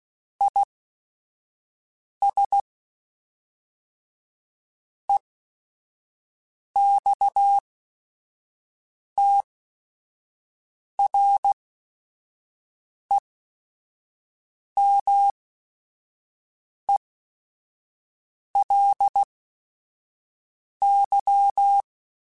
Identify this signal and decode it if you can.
Morse code, heard as a keyed tone, about 16 wpm, spaced as for 5 wpm, 800 Hz. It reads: IS EXTREMELY
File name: Technical Subjects / Technical Subjects-mp3s / AND Gate @5